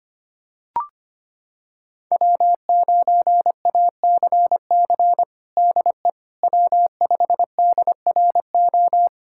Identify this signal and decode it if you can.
Morse code, heard as a keyed tone, about 25 wpm, 700 Hz. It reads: W9ACC DE W5DRO
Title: USA Callsign DE Callsign - 5 Characters - 25wpm